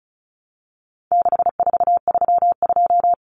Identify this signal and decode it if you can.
Morse code, heard as a keyed tone, about 34 wpm, 700 Hz. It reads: 6432
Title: Common Cut Numbers - 4-Digit 35wpm